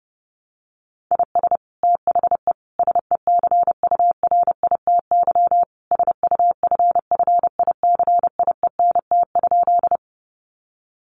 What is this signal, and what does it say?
Morse code, heard as a keyed tone, about 30 wpm, 700 Hz. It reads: IS THE SECURITY SUFFICIENT?
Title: Sentences from Top 1400 Words 30wpm